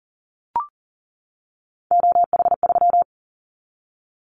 Morse code, heard as a keyed tone, about 41 wpm, 700 Hz. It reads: OH3
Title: Callsign Prefixes for Top 50 DXCC - 40wpm